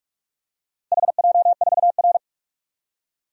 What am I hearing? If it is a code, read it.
Morse code, heard as a keyed tone, about 46 wpm, 700 Hz. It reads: HJ4R
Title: DXCC-100 Call Signs - 4 Characters - Repeated 2x - 45wpm